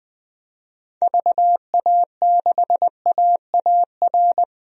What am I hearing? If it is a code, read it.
Morse code, heard as a keyed tone, about 20 wpm, 700 Hz. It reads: VA6AAR
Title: DXCC-100 Call Signs - 6 Characters - 20wpm